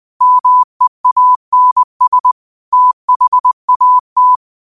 Morse code, heard as a keyed tone, about 20 wpm, 1000 Hz. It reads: MEANS THAT